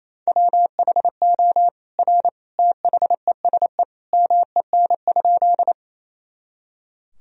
Morse code, heard as a keyed tone, about 28 wpm, 700 Hz. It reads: WHO R THESE MEN?
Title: Sentences from Top 200 Words and Abbreviations 28wpm